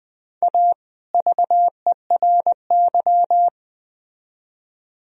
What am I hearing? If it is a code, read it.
Morse code, heard as a keyed tone, about 20 wpm, 700 Hz. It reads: A VERY
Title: Speed-Racing - Sets of 2 Words 17wpm